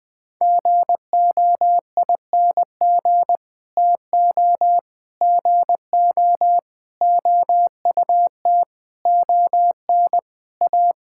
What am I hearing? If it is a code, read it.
Morse code, heard as a keyed tone, about 20 wpm, 700 Hz. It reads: GOING TO GO OUT ON A
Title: Sets of 8 Words - 20wpm